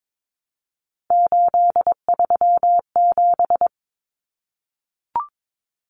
Morse code, heard as a keyed tone, about 22 wpm, 700 Hz. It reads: 837
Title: Numbers - 3-Digit 22wpm